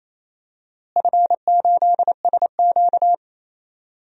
Morse code, heard as a keyed tone, about 28 wpm, 700 Hz. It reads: F8SQ
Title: DXCC-100 Call Signs - 4 Characters - Repeated 2x - 28wpm